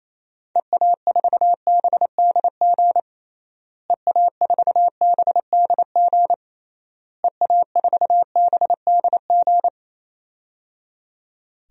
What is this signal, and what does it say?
Morse code, heard as a keyed tone, about 28 wpm, 700 Hz. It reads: EA4BDG EA4BDG EA4BDG
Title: DXCC-100 Call Signs - 6 Characters - Repeated 3x - 28wpm